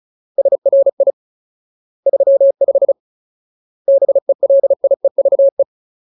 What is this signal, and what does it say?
Morse code, heard as a keyed tone, about 34 wpm, 550 Hz. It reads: SRI 35 BELIEVE
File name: Ultimate Word List / 35WPM Ultimate